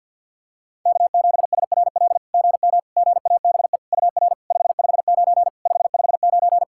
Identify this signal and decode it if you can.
Morse code, heard as a keyed tone, about 50 wpm, 690 Hz. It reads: K7SRL GM GABE UR 559 559
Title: SOTA QSO Practice 50wpm